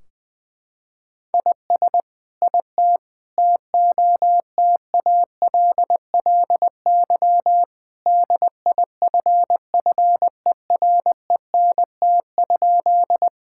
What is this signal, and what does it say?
Morse code, heard as a keyed tone, about 20 wpm, 700 Hz. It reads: IS IT TOTALLY DIFFERENT?
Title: Sentences from Top 1600 Words 20wpm